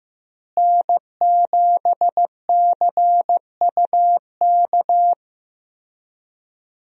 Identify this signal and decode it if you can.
Morse code, heard as a keyed tone, about 16 wpm, 700 Hz. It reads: N7CUK
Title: USA Call Signs - 5 Characters - Repeated 3x - 15wpm